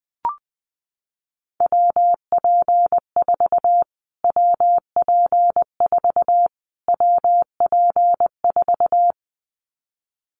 Morse code, heard as a keyed tone, about 19 wpm, 700 Hz. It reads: WP4 WP4 WP4
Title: Callsign Prefixes for Top 25 DXCC - Repeated 3x - 20wpm